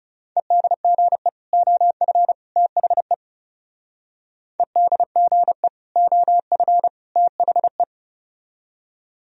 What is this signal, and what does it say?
Morse code, heard as a keyed tone, about 39 wpm, 700 Hz. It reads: EDGE OF THE EDGE OF THE